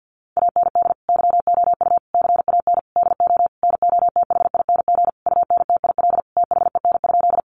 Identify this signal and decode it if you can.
Morse code, heard as a keyed tone, about 50 wpm, 700 Hz. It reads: AND YOU CAN DO NOTHING UNTIL THEN?
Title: Sentences from Top 500 Words 50wpm